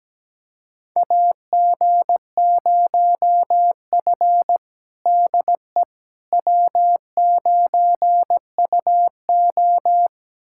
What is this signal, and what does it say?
Morse code, heard as a keyed tone, about 17 wpm, 700 Hz. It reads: AG0F DE W9UO
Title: USA Callsign DE Callsign - 4 Characters - 17wpm